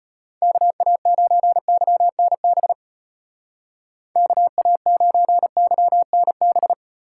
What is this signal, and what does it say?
Morse code, heard as a keyed tone, about 33 wpm, 700 Hz. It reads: KA9YNB KA9YNB